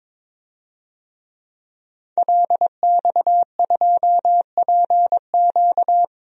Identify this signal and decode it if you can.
Morse code, heard as a keyed tone, about 22 wpm, 700 Hz. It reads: LX2PQ
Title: DXCC-100 Call Signs - 5 Characters - 22wpm